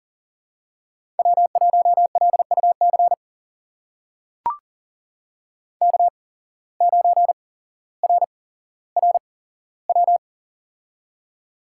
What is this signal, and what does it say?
Morse code, heard as a keyed tone, about 40 wpm, spaced as for 17 wpm, 700 Hz. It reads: W1LUC K9RRW
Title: USA Call Signs - 5 Characters - 8x Character Spacing - 40wpm